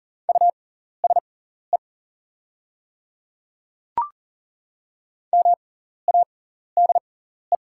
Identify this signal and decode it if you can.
Morse code, heard as a keyed tone, about 40 wpm, spaced as for 14 wpm, 690 Hz. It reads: USE MADE
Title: 6x Character Spacing - Top 200 Words - 40wpm